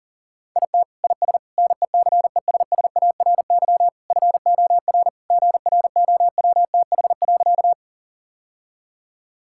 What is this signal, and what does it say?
Morse code, heard as a keyed tone, about 40 wpm, 700 Hz. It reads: IT IS NECESSARY FOR GROWTH.
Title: Sentences from Top 1000 Words - Review Part 5 - 40wpm